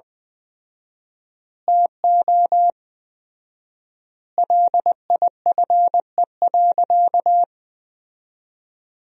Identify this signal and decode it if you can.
Morse code, heard as a keyed tone, about 21 wpm, 700 Hz. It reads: TO LIFE.
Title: 4x Word-Spacing - Sentences from Top 400 Words - 20wpm